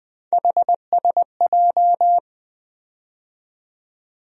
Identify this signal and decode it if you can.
Morse code, heard as a keyed tone, about 20 wpm, 700 Hz. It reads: HSJ
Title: USA Call Sign Postfixes - 3 Letters 20wpm